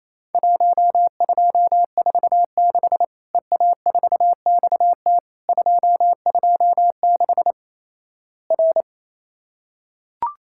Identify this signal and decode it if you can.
Morse code, heard as a keyed tone, about 28 wpm, 700 Hz. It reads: 1246 EA4XT 226 R
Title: Worked All Europe DX Contest - QTC Exchange - 28wpm